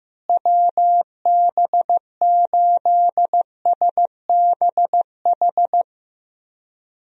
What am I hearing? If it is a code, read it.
Morse code, heard as a keyed tone, about 15 wpm, 700 Hz. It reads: WB8SBH
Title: USA Call Signs - 6 Characters - Repeated 3x - 15wpm